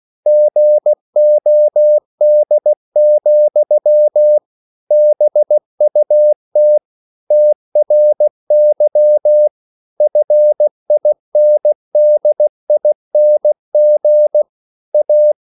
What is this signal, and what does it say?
Morse code, heard as a keyed tone, about 16 wpm, 600 Hz. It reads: GOD, BUT TRY FINDING A